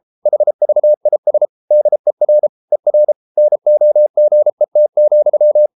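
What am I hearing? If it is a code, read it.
Morse code, heard as a keyed tone, about 33 wpm, 600 Hz. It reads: HVIS DER ER NOGET,
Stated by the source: Morse code at 33 wpm, QRQ.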